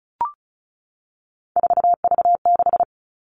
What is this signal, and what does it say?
Morse code, heard as a keyed tone, about 34 wpm, 700 Hz. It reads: 4V6
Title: All Cut Numbers - 3-Digit 35wpm